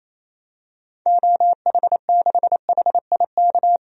Morse code, heard as a keyed tone, about 28 wpm, 700 Hz. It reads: OH6HIK
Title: DXCC-100 Call Signs - 6 Characters - 28wpm